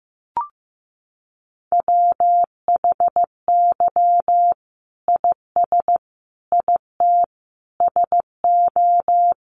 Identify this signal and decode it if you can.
Morse code, heard as a keyed tone, about 15 wpm, 700 Hz. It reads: WHY IS IT SO
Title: Sentences from Top 1500 Words 15wpm